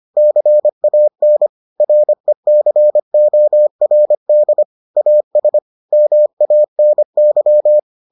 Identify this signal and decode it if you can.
Morse code, heard as a keyed tone, about 25 wpm, 600 Hz. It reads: CAN RECORD AS MANY